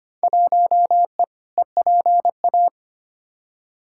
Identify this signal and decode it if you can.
Morse code, heard as a keyed tone, about 25 wpm, 700 Hz. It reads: 1E EPA
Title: Field Day Exchanges - 25wpm